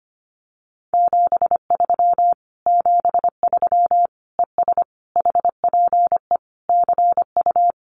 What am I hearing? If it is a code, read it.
Morse code, heard as a keyed tone, about 25 wpm, 700 Hz. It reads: 73 73 ES HPE CU
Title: Common QSO Phrases Encore 25wpm